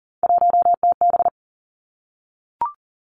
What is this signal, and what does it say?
Morse code, heard as a keyed tone, about 37 wpm, 700 Hz. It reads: 1TB
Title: Three Challenging Letter-Number Combinations 40wpm